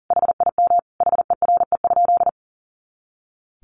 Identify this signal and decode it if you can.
Morse code, heard as a keyed tone, about 40 wpm, 700 Hz. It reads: HIM HERE?
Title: Sentences from Top 500 Words 40wpm